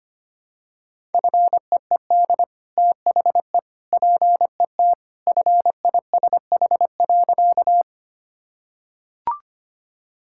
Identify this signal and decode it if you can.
Morse code, heard as a keyed tone, about 25 wpm, 690 Hz. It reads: FEED THE PET FISH.
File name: Easy Sentences with Hints - 25wpm